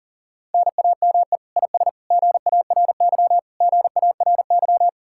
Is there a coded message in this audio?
Yes: Morse code from 0.5 to 4.9 s, heard as a keyed tone, about 52 wpm, 700 Hz. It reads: NAME IS GARY GARY